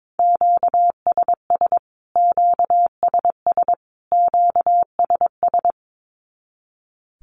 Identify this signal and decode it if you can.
Morse code, heard as a keyed tone, about 22 wpm, 700 Hz. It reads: QSS QSS QSS
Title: Q-Codes Repeated 3x 22wpm